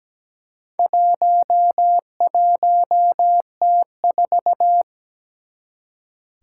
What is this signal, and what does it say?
Morse code, heard as a keyed tone, about 17 wpm, 700 Hz. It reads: 11T4
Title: Four Challenging Letter-Number Combinations 17wpm